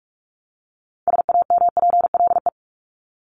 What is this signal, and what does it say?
Morse code, heard as a keyed tone, about 44 wpm, 700 Hz. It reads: SAMPLE
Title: Top 1501-1750 Words 45wpm